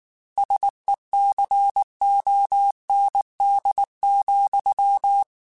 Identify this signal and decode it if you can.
Morse code, heard as a keyed tone, about 19 wpm, 800 Hz. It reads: SECOND,
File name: Technical Subjects / Technical Subjects-mp3s / Reactance @18